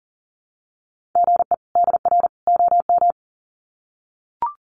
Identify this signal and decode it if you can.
Morse code, heard as a keyed tone, about 40 wpm, 700 Hz. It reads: GE DR OM
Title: Common QSO Phrases 40wpm